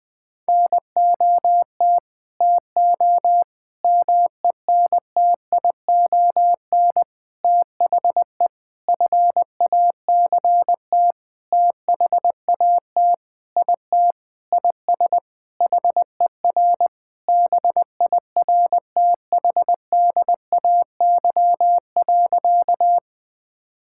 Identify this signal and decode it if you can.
Morse code, heard as a keyed tone, about 19 wpm, 700 Hz. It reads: NOT TO MENTION THE FACT THAT IT IS HER BIRTHDAY.
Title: Sentences with 6-Word Phrases - 20wpm